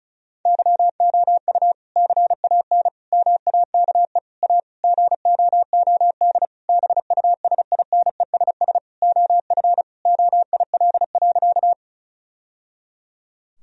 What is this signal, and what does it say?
Morse code, heard as a keyed tone, about 34 wpm, 700 Hz. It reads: YOU CAN MAKE A GOOD BUSINESS OF OIL.
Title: Sentences from Top 1000 Words - Review Part 4 - 35wpm